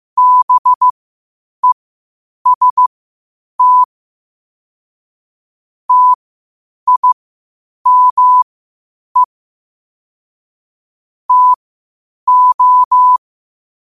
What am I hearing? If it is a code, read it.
Morse code, heard as a keyed tone, about 15 wpm, spaced as for 8 wpm, 1000 Hz. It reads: BEST TIME TO
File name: qotdfiles / 8wpm